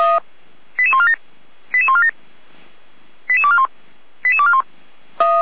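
Five-tone selective call (ZVEI), heard as quick runs of five tones, tones of about 70 ms. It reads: 88147 88147 88241 88241
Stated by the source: ZVEI five-tone call received by SDR.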